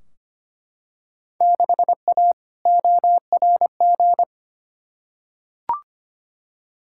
Morse code, heard as a keyed tone, about 25 wpm, 700 Hz. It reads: 6A ORG
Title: Field Day Exchanges Encore - 25wpm